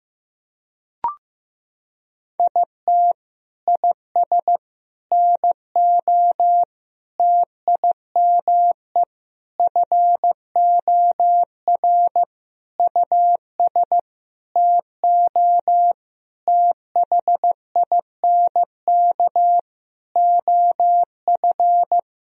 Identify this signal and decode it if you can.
Morse code, heard as a keyed tone, about 15 wpm, 700 Hz. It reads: IT IS NO TIME FOR US TO THINK OF
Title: Sentences from Top 1000 Words - Review Part 3 - 15wpm